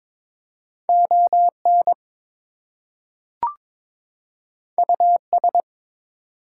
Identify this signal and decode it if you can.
Morse code, heard as a keyed tone, about 22 wpm, 700 Hz. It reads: ON US
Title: Two Letter Words - 22wpm